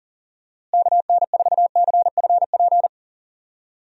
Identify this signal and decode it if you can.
Morse code, heard as a keyed tone, about 37 wpm, 700 Hz. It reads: KN4CFP
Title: USA Call Signs - 6 Characters - Repeated 3x - 40wpm